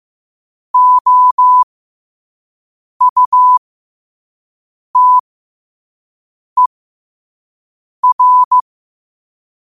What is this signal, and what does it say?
Morse code, heard as a keyed tone, about 15 wpm, spaced as for 5 wpm, 1000 Hz. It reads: OUTER
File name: qotdfiles / 5wpm